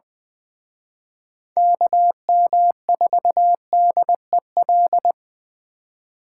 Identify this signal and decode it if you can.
Morse code, heard as a keyed tone, about 24 wpm, 700 Hz. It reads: KM4DEL
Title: Speed Racing - USA 2x3 Call Signs 20wpm